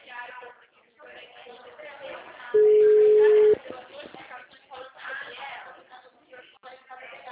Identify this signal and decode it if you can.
Ringback tone (Europe, 425 Hz), with speech.